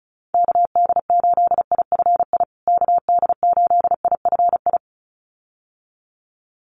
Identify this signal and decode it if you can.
Morse code, heard as a keyed tone, about 34 wpm, 700 Hz. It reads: KD8IFI KD8IFI
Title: USA Call Signs Repeated 2x 35wpm